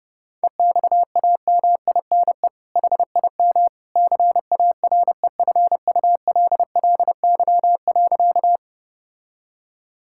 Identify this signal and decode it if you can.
Morse code, heard as a keyed tone, about 30 wpm, 700 Hz. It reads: EXAMINE HIM CAREFULLY.
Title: Sentences from Top 1600 Words 30wpm